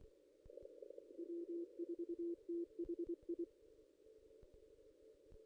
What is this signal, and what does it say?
Morse code, heard as a keyed tone, about 25 wpm, 340 Hz. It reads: W4THI
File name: contests / ss-cw-k5tr-op-n5rz -1800